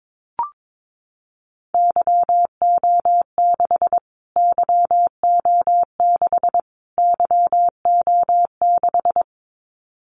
Morse code, heard as a keyed tone, about 22 wpm, 700 Hz. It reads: YO6 YO6 YO6
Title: Callsign Prefixes for Top 25 DXCC - Repeated 3x - 22wpm